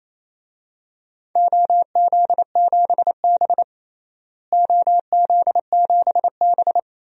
Morse code, heard as a keyed tone, about 28 wpm, 700 Hz. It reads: OZ7B OZ7B
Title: DXCC-100 Call Signs - 4 Characters - Repeated 3x - 28wpm